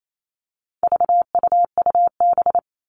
Morse code, heard as a keyed tone, about 28 wpm, 700 Hz. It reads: VUUB
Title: All Cut Numbers - 4-Digit 28wpm